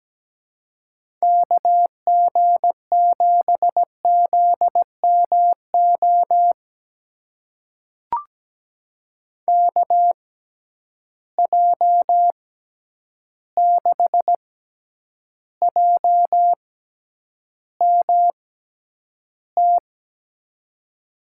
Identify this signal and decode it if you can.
Morse code, heard as a keyed tone, about 17 wpm, spaced as for 9 wpm, 700 Hz. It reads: KG7ZMO KJ6JMT
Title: USA Call Signs - 6 Characters - 6x Character Spacing - 17wpm